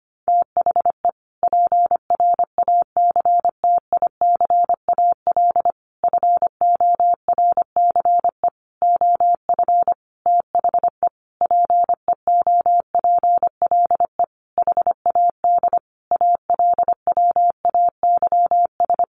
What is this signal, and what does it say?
Morse code, heard as a keyed tone, about 25 wpm, 700 Hz. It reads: THE PRACTICAL FORCE OF THE PEOPLE HAD ALWAYS
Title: Sentences from Top 1400 Words 25wpm